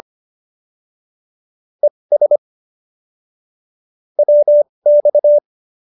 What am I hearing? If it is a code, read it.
Morse code, heard as a keyed tone, about 25 wpm, spaced as for 20 wpm, 590 Hz. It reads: ES WX